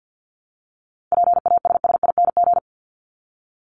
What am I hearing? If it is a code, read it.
Morse code, heard as a keyed tone, about 50 wpm, 700 Hz. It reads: PASSING
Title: ING Suffix - Top 100 Words with Base Words 50wpm